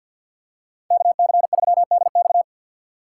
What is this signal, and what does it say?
Morse code, heard as a keyed tone, about 52 wpm, 700 Hz. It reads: KC3DX
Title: USA Call Signs - 5 Characters - 8x Character Spacing - 50wpm